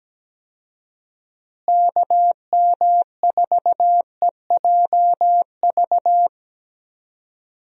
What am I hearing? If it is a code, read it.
Morse code, heard as a keyed tone, about 17 wpm, 700 Hz. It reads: KM4EJV
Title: USA Call Signs - 6 Characters - Repeated 2x - 17wpm